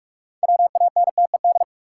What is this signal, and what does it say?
Morse code, heard as a keyed tone, about 41 wpm, 700 Hz. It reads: WANTED
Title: Sets of 2 Words - New Word Drill - 45wpm